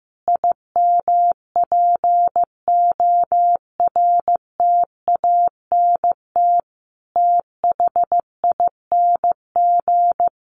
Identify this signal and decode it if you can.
Morse code, heard as a keyed tone, about 15 wpm, 700 Hz. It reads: IMPORTANT THING
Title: Sets of 5 Words Encore 15wpm